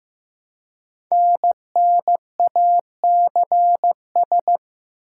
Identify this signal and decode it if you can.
Morse code, heard as a keyed tone, about 15 wpm, 700 Hz. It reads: NNACS
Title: Code Groups - 5 Letters - 15wpm